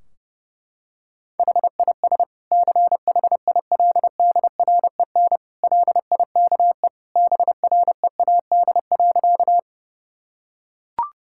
Morse code, heard as a keyed tone, about 30 wpm, 700 Hz. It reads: HIS CHILDREN LIKE BREAD.
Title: Sentences from Top 1600 Words 30wpm